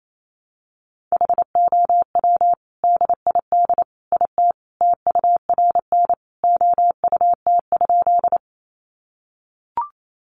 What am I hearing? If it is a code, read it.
Morse code, heard as a keyed tone, about 28 wpm, 700 Hz. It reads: HOW DID IT TURN OUT?
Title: A Sound Of Thunder 28wpm